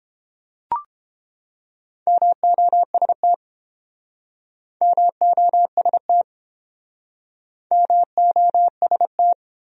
Morse code, heard as a keyed tone, about 31 wpm, 700 Hz. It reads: MOST MOST MOST